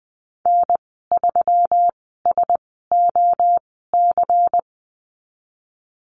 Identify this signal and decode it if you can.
Morse code, heard as a keyed tone, about 20 wpm, spaced as for 14 wpm, 700 Hz. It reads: N3SOC
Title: 147 Lesson 24g - Letter y - Call Signs 20wpm